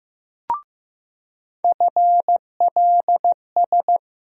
Morse code, heard as a keyed tone, about 15 wpm, 700 Hz. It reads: FLS
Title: USA Call Sign Postfixes - 3 Letters 15wpm